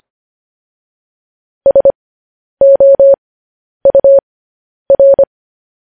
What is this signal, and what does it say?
Morse code, heard as a keyed tone, about 25 wpm, spaced as for 10 wpm, 550 Hz. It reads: SOUR
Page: news headlines in morse code at 10 wpm